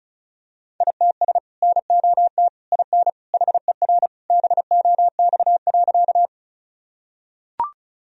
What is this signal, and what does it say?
Morse code, heard as a keyed tone, about 36 wpm, 700 Hz. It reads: ITS NOT IN HER BOX.
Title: Sentences from Top 1000 Words - Review Part 2 - 35wpm